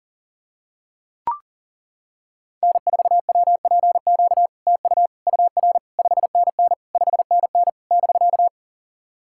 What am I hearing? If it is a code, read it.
Morse code, heard as a keyed tone, about 40 wpm, 700 Hz. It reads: N4WPQ TU UR 5NN 5NN <BK>
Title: SOTA QSO Practice 40wpm